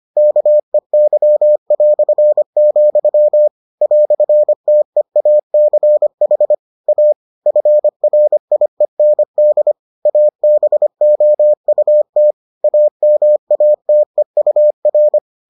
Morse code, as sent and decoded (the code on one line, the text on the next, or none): -.- . -.-- .-..-. --..-- / .-..-. - . .- -.-. .... / .- / ..-. .-. .. -. -.. / .- -... --- ..- - / .- -- .- - . ..- .-.
KEY", "TEACH A FRIEND ABOUT AMATEUR